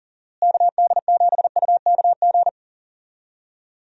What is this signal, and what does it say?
Morse code, heard as a keyed tone, about 40 wpm, 700 Hz. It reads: KD7UKG
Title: USA Call Signs - 6 Characters - Repeated 2x - 40wpm